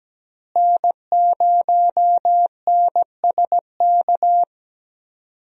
Morse code, heard as a keyed tone, about 17 wpm, 700 Hz. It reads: N0NSK
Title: USA Call Signs - 5 Characters - Repeated 2x - 17wpm